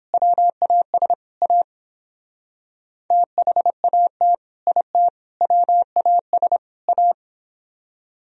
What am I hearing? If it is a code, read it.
Morse code, heard as a keyed tone, about 29 wpm, 700 Hz. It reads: WAS A THAT IT WAS A